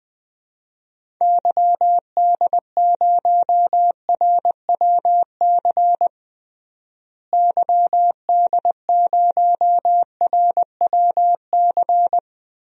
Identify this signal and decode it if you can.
Morse code, heard as a keyed tone, about 20 wpm, 700 Hz. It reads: YD0RWC YD0RWC
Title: DXCC-100 Call Signs - 6 Characters - Repeated 2x - 20wpm